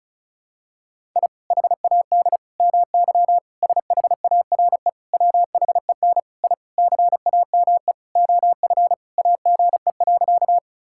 Morse code, heard as a keyed tone, about 36 wpm, 700 Hz. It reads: I HAD MY SHARE WHEN I CAME OF AGE.
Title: Sentences from Top 1000 Words - Review Part 6 - 35wpm